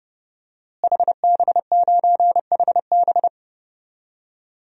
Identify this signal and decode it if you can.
Morse code, heard as a keyed tone, about 30 wpm, 700 Hz. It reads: HB9HB
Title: DXCC-100 Call Signs - 5 Characters - Repeated 2x - 30wpm